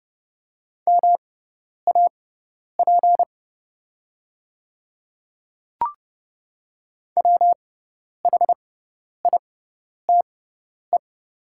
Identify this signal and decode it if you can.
Morse code, heard as a keyed tone, about 30 wpm, spaced as for 10 wpm, 700 Hz. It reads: MAP WHITE
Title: 6x Character Spacing - Top 500 Words - 30wpm